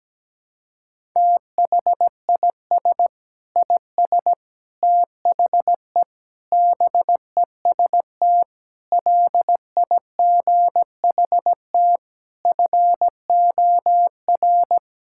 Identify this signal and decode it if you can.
Morse code, heard as a keyed tone, about 17 wpm, 700 Hz. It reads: THIS IS THE BEST LIGHT FOR
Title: Sentences from Top 1000 Words - Review Part 6 - 17wpm